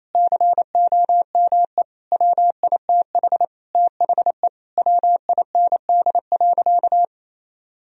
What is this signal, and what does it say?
Morse code, heard as a keyed tone, about 28 wpm, 700 Hz. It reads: COME WITH THE WIND.
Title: Sentences from Top 1000 Words - Review Part 2 - 28wpm